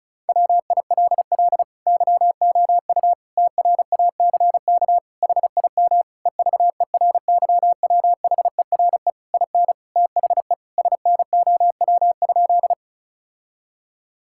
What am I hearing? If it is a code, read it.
Morse code, heard as a keyed tone, about 36 wpm, 700 Hz. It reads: WILL YOU TRACK HIM EVERYWHERE IN THE SNOW?
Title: Sentences from Top 1600 Words 35wpm